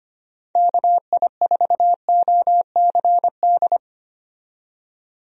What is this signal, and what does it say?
Morse code, heard as a keyed tone, about 25 wpm, 700 Hz. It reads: KI4OCD
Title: USA Call Signs - 6 Characters - 25wpm